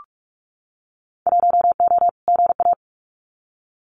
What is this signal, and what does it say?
Morse code, heard as a keyed tone, about 46 wpm, 700 Hz. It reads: 1O GA
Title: Winter Field Day Exchanges 45wpm